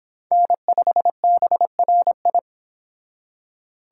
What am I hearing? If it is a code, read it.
Morse code, heard as a keyed tone, about 26 wpm, 700 Hz. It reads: N5BRI